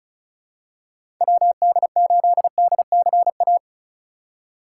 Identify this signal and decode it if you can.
Morse code, heard as a keyed tone, about 36 wpm, 700 Hz. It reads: WD8DCA
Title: USA Call Signs - 6 Characters - 3x Character Spacing - 35wpm